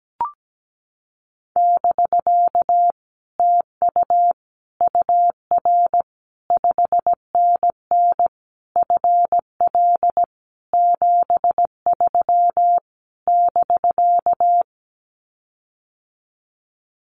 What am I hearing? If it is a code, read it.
Morse code, heard as a keyed tone, about 17 wpm, 700 Hz. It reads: <BK> TU UR 5NN FL 73 <BK>
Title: Common QSO Phrases Encore 17wpm